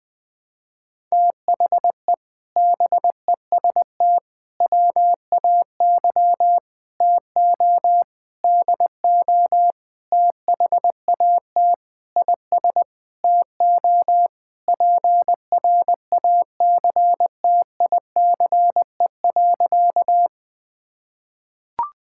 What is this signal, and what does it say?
Morse code, heard as a keyed tone, about 21 wpm, 700 Hz. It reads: THE BEST WAY TO DO THAT IS TO PRACTICE.
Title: Sentences with 8-Word Phrases - 20wpm